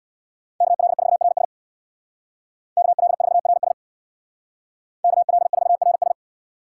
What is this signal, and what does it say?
Morse code, heard as a keyed tone, about 61 wpm, 700 Hz. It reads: KL4RS KL4RS KL4RS
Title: Speed Racing - USA 2x2 Call Signs 50wpm